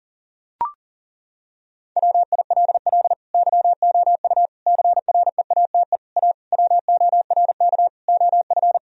Morse code, heard as a keyed tone, about 40 wpm, 700 Hz. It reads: WILL YOU CREATE A WORK OF
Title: Sentences from Top 1000 Words - Review Part 3 - 40wpm